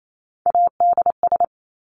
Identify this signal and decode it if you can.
Morse code, heard as a keyed tone, about 28 wpm, 700 Hz. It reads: ADS
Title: Abbreviations 28wpm